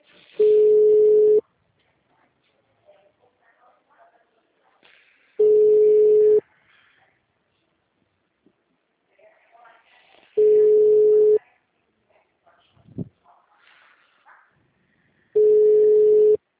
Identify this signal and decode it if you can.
Ringback tone (Europe, 425 Hz).